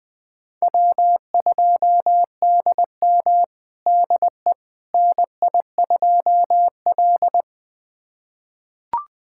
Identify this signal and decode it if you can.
Morse code, heard as a keyed tone, about 21 wpm, 700 Hz. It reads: W2DM DE NI2L
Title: USA Callsign DE Callsign - 4 Characters - 20wpm